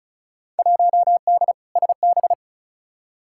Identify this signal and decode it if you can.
Morse code, heard as a keyed tone, about 34 wpm, 700 Hz. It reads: 1D SB
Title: Field Day Exchanges - 35wpm